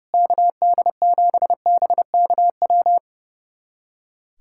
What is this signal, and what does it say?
Morse code, heard as a keyed tone, about 30 wpm, 700 Hz. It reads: KD7BKW
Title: USA Callsign DE Callsign - 6 Characters - 30wpm